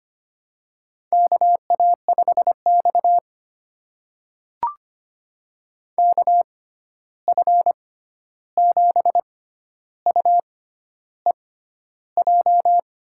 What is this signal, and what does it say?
Morse code, heard as a keyed tone, about 25 wpm, spaced as for 11 wpm, 700 Hz. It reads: KA5X KF7UEJ
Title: USA Call Signs - 6x Character Spacing - 25wpm